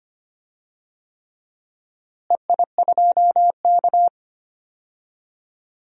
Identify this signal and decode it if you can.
Morse code, heard as a keyed tone, about 25 wpm, 700 Hz. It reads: EI2K
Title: DXCC-100 Call Signs - 4 Characters - Repeated 3x - 25wpm